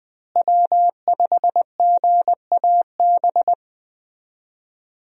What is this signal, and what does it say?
Morse code, heard as a keyed tone, about 20 wpm, 700 Hz. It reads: W5GAB
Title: USA Call Signs - 5 Characters - Repeated 2x - 20wpm